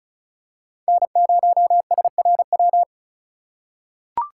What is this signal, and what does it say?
Morse code, heard as a keyed tone, about 36 wpm, 700 Hz. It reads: N0SRW
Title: USA Call Signs - 3x Character Spacing - 35wpm